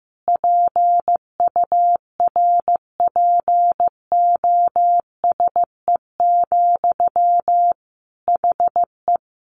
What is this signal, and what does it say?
Morse code, heard as a keyed tone, about 15 wpm, 700 Hz. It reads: PURPOSE, HE
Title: Sentences from Top 800 Words 15wpm